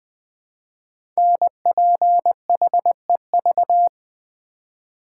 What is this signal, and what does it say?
Morse code, heard as a keyed tone, about 20 wpm, 700 Hz. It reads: NPHEV
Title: Code Groups - 5 Letters - 20wpm